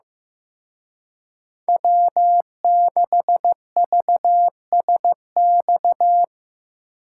Morse code, heard as a keyed tone, about 18 wpm, 700 Hz. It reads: W6VSX
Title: Speed Racing - USA 1x3 Call Signs 15wpm